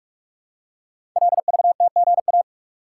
Morse code, heard as a keyed tone, about 44 wpm, 700 Hz. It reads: LVTGA
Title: Code Groups - 5 Letters - 45wpm